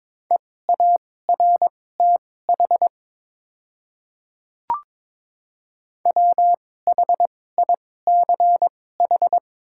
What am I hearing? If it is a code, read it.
Morse code, heard as a keyed tone, about 22 wpm, spaced as for 16 wpm, 700 Hz. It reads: EARTH WHICH
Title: 2x Character Spacing - Top 200 Words - 22wpm